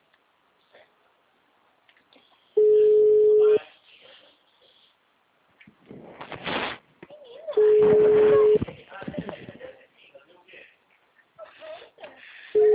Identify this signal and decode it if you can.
Ringback tone (Europe, 425 Hz), with speech.